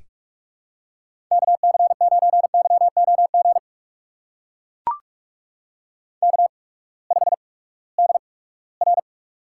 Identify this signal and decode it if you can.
Morse code, heard as a keyed tone, about 44 wpm, spaced as for 22 wpm, 700 Hz. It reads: KC9YOG K5DR
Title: USA Call Signs - 8x Character Spacing - 45wpm